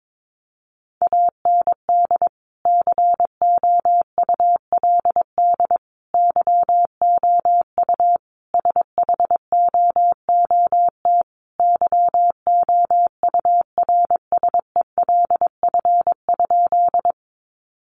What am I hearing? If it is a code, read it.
Morse code, heard as a keyed tone, about 22 wpm, 700 Hz. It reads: AND COULD YOU SHOOT YOURSELF?
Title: Sentences from Top 1200 Words 22wpm